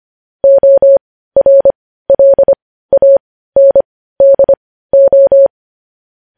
Morse code, heard as a keyed tone, about 25 wpm, spaced as for 15 wpm, 550 Hz. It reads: ORLANDO
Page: news headlines in morse code at 15 wpm